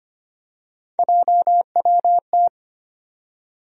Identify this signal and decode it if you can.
Morse code, heard as a keyed tone, about 25 wpm, 700 Hz. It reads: JWT
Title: USA Call Sign Postfixes - 3 Letters 25wpm